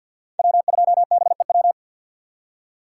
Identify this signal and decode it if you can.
Morse code, heard as a keyed tone, about 46 wpm, 700 Hz. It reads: W2BEW
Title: USA Call Signs - 5 Characters - 4x Character Spacing - 50wpm